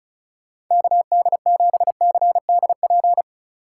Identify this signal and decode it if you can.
Morse code, heard as a keyed tone, about 40 wpm, 700 Hz. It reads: KD7CDP